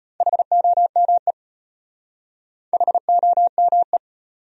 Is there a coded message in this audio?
Yes: Morse code from 0.2 to 4.0 s, heard as a keyed tone, about 32 wpm, 700 Hz. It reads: HOME HOME